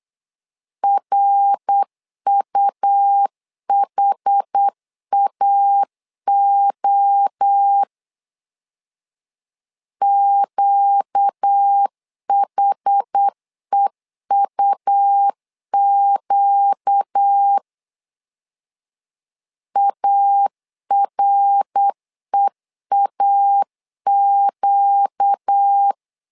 Morse code, heard as a keyed tone, about 8 wpm, 800 Hz. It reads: RUHAO QHEUQ AREAQ
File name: 4 Q 8wpm